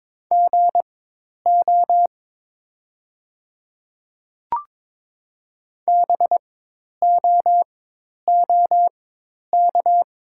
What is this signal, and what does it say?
Morse code, heard as a keyed tone, about 23 wpm, spaced as for 10 wpm, 700 Hz. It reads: GO BOOK
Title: 4x Character Spacing - Top 300 Words - 22wpm